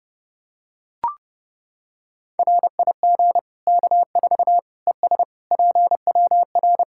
Morse code, heard as a keyed tone, about 32 wpm, 700 Hz. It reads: RIG K4 ES PWR
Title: K1USN SST QSO Practice 30wpm